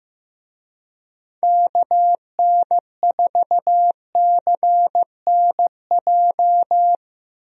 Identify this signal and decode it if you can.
Morse code, heard as a keyed tone, about 15 wpm, 700 Hz. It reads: KN4CNJ
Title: USA Call Signs - 6 Characters - 15wpm